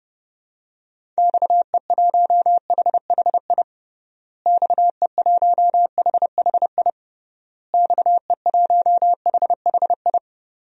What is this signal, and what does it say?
Morse code, heard as a keyed tone, about 30 wpm, 700 Hz. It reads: XE1HHI XE1HHI XE1HHI
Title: DXCC-100 Call Signs - 6 Characters - Repeated 3x - 30wpm